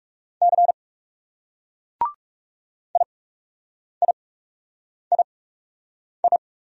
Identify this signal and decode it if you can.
Morse code, heard as a keyed tone, about 38 wpm, spaced as for 8 wpm, 700 Hz. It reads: C IIII